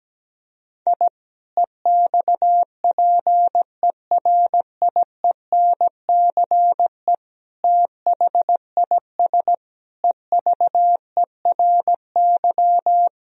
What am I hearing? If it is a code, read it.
Morse code, heard as a keyed tone, about 17 wpm, 700 Hz. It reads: I EXPERIENCE THIS EVERY
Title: Sentences from Top 1000 Words - Review Part 3 - 17wpm